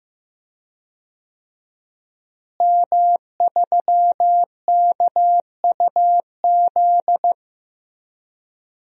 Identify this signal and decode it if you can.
Morse code, heard as a keyed tone, about 15 wpm, 700 Hz. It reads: M3KUZ
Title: DXCC-100 Call Signs - 5 Characters - Repeated 2x - 15wpm